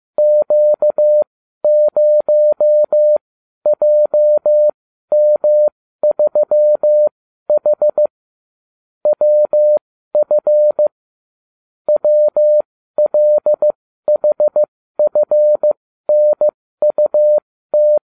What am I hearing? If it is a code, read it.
Morse code, heard as a keyed tone, about 15 wpm, spaced as for 12 wpm, 600 Hz. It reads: Q0JM3H WF WLHFNUT